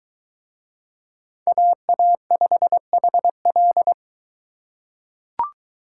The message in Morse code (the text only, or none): AA5HL